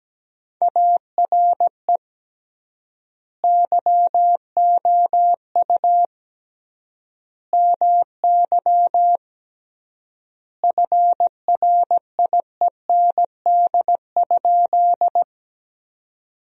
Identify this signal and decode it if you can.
Morse code, heard as a keyed tone, about 17 wpm, 700 Hz. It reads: ARE YOU MY FRIEND?
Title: 3x Word-Spacing - Sentences from Top 300 Words - 17wpm